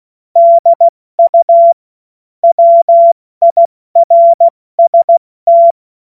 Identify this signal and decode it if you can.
Morse code, heard as a keyed tone, about 16 wpm, 680 Hz. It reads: DU WIRST
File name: Morsekurs / Witze / Witz 09. August 2022 70 F80